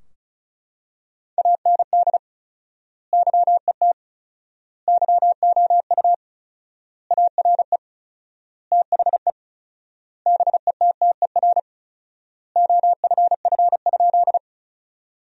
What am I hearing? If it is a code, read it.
Morse code, heard as a keyed tone, about 34 wpm, 700 Hz. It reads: AND YET YOU ARE THE BETTER OFF?
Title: 4x Word-Spacing - Sentences from Top 500 Words - 35wpm